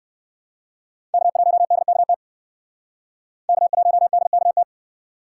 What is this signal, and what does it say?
Morse code, heard as a keyed tone, about 60 wpm, 700 Hz. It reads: K1DKT K1DKT